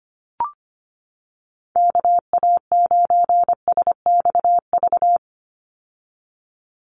Morse code, heard as a keyed tone, about 25 wpm, 700 Hz. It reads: KA9SXV
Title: Call Signs - 25wpm